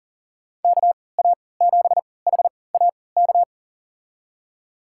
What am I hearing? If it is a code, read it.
Morse code, heard as a keyed tone, about 40 wpm, spaced as for 23 wpm, 700 Hz. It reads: KA7HAK
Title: USA Call Signs - 6 Characters - 3x Character Spacing - 40wpm